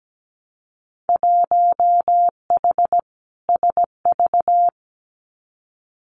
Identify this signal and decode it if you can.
Morse code, heard as a keyed tone, about 17 wpm, 700 Hz. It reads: 1H SV
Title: Winter Field Day Exchanges 17wpm